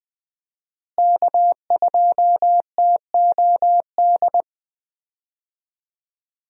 Morse code, heard as a keyed tone, about 20 wpm, 700 Hz. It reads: K2TOD
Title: USA Call Signs - 5 Characters - Repeated 2x - 20wpm